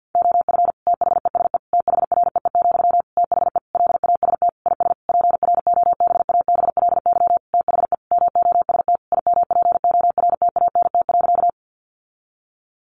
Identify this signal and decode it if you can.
Morse code, heard as a keyed tone, about 48 wpm, 700 Hz. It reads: OF THESE THREE, THE LAST IS PROBABLY THE MOST IMPORTANT.
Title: Sentences from Top 700 Words 50wpm